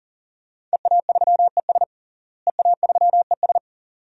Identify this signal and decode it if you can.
Morse code, heard as a keyed tone, about 40 wpm, 700 Hz. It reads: EA3ES EA3ES
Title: DXCC-100 Call Signs - 5 Characters - Repeated 2x - 40wpm